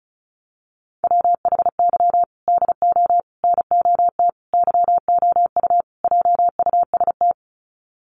Morse code, heard as a keyed tone, about 34 wpm, 700 Hz. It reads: WHY DO NOT YOU JUST
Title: Sets of 5 Words 35wpm